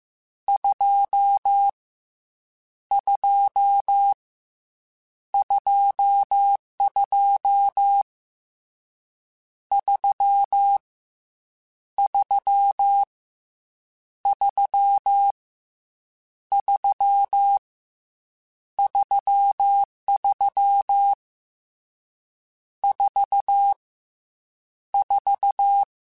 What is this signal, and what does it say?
Morse code, heard as a keyed tone, about 15 wpm, 800 Hz. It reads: 2 2 22 3 3 3 3 33 4 4